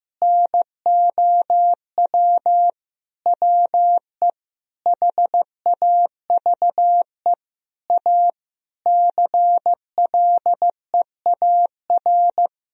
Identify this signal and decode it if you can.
Morse code, heard as a keyed tone, about 16 wpm, 700 Hz. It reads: NOW WE HAVE A CLEAR
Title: Sentences from Top 1000 Words - Review Part 2 - 15wpm